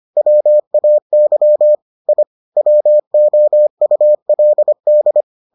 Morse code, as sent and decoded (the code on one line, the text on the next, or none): .-- .- -.-- / .. / .-- --- ..- .-.. -..
WAY I WOULD